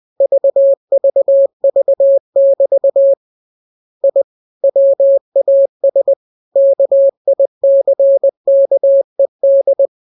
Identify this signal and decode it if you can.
Morse code, heard as a keyed tone, about 20 wpm, 550 Hz. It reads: VVV<BT> I WAS KICKED
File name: jokes / joke 3 20 0000